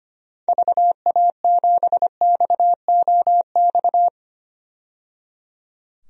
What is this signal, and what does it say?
Morse code, heard as a keyed tone, about 25 wpm, 700 Hz. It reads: VA7XOX
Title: DXCC-100 Call Signs - 6 Characters - 25wpm